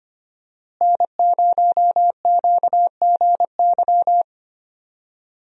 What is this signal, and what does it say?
Morse code, heard as a keyed tone, about 25 wpm, 700 Hz. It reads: N0QGY
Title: USA Call Signs - 5 Characters - Repeated 2x - 25wpm